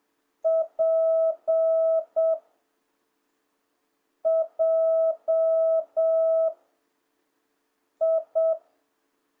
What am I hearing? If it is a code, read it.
Morse code, heard as a keyed tone, about 7 wpm, 660 Hz. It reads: P J I